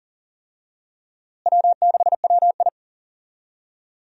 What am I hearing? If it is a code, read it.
Morse code, heard as a keyed tone, about 37 wpm, 700 Hz. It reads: W6WI
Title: USA Call Signs - 4 Characters - Repeated 2x - 40wpm